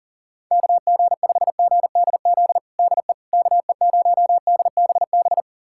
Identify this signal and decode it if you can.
Morse code, heard as a keyed tone, about 40 wpm, 700 Hz. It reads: KG5GDZ DE KE0DBB
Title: USA Callsign DE Callsign - 6 Characters - 40wpm